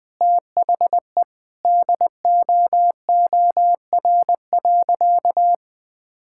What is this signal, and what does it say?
Morse code, heard as a keyed tone, about 21 wpm, 700 Hz. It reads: THE DOOR.
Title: Sentences with 5-Word Phrases - 20wpm